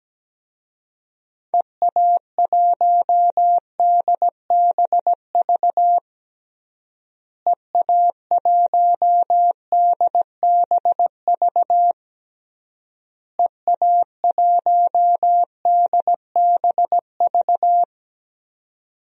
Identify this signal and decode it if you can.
Morse code, heard as a keyed tone, about 17 wpm, 700 Hz. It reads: EA1DBV EA1DBV EA1DBV
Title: DXCC-100 Call Signs - 6 Characters - Repeated 3x - 17wpm